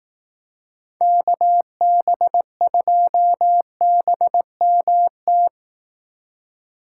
Morse code, heard as a keyed tone, about 18 wpm, 700 Hz. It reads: KB2BMT
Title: Speed Racing - USA 2x3 Call Signs 15wpm